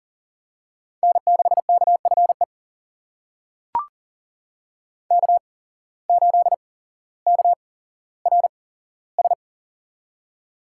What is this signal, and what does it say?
Morse code, heard as a keyed tone, about 40 wpm, spaced as for 17 wpm, 700 Hz. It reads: N6KFE K8KRS
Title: USA Call Signs - 5 Characters - 8x Character Spacing - 40wpm